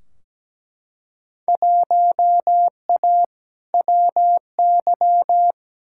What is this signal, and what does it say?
Morse code, heard as a keyed tone, about 17 wpm, 700 Hz. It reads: 1A WY
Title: Field Day Exchanges - 17wpm